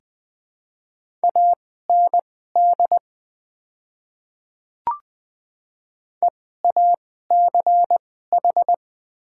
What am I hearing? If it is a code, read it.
Morse code, heard as a keyed tone, about 20 wpm, spaced as for 15 wpm, 700 Hz. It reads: AND EACH
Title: 2x Character Spacing - Top 100 Words - 20wpm